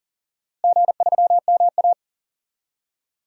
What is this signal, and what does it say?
Morse code, heard as a keyed tone, about 40 wpm, 700 Hz. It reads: G3MA
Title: DXCC-100 Call Signs - 4 Characters - Repeated 2x - 40wpm